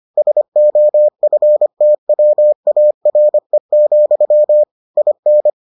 Morse code, heard as a keyed tone, about 25 wpm, 600 Hz. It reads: SOFTWARE, IN